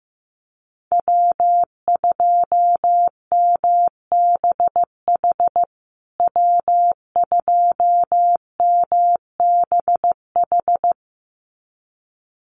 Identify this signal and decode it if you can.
Morse code, heard as a keyed tone, about 15 wpm, 700 Hz. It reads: W2MBH W2MBH
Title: USA Call Signs Repeated 2x 15wpm